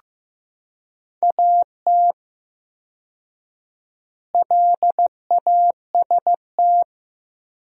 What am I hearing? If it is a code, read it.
Morse code, heard as a keyed tone, about 15 wpm, 700 Hz. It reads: AT LAST
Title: 4x Word-Spacing - Sentences from Top 300 Words - 15wpm